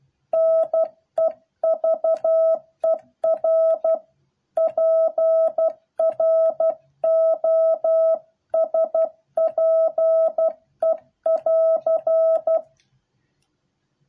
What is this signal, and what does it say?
Morse code, heard as a keyed tone, about 12 wpm, 650 Hz. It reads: NEVER PROSPE<AR>